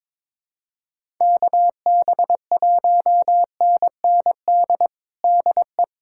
Morse code, heard as a keyed tone, about 21 wpm, 700 Hz. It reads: KB1NND DE
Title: USA Callsign DE Callsign - 6 Characters - 22wpm